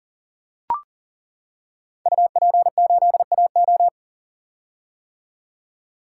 Morse code, heard as a keyed tone, about 39 wpm, 700 Hz. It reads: UP8AO
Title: Code Groups - 5 Letter:Numbers - 40wpm